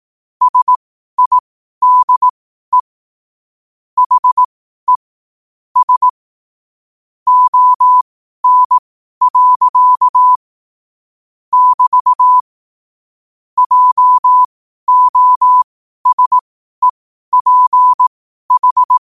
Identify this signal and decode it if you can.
Morse code, heard as a keyed tone, about 18 wpm, spaced as for 12 wpm, 1000 Hz. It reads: SIDE HE S ON. <BT> JOSEPH
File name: qotdfiles / 12wpm